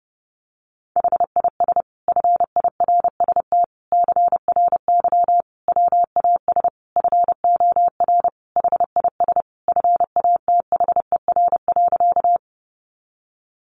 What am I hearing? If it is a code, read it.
Morse code, heard as a keyed tone, about 30 wpm, 700 Hz. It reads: HIS FIRST CRY WAS FOR HIS FATHER.
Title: Sentences from Top 500 Words 30wpm